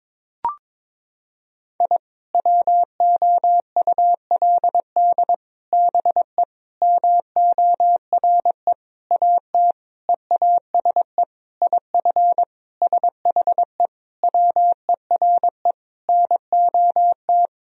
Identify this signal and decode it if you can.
Morse code, heard as a keyed tone, about 22 wpm, 700 Hz. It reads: I WOULD BE MORE AT EASE IF SHE WERE NOT
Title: Sentences from Top 1000 Words - Review Part 1 - 22wpm